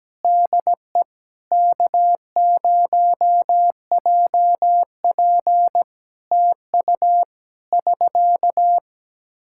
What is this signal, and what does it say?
Morse code, heard as a keyed tone, about 17 wpm, 700 Hz. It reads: DE K0JP TU <SK>
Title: Practice QSO - Exchange 4 - Ending - 17wpm